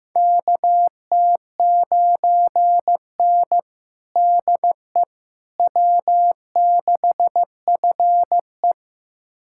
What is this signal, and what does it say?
Morse code, heard as a keyed tone, about 15 wpm, 700 Hz. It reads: KT9N DE W6FE
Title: USA Callsign DE Callsign - 4 Characters - 15wpm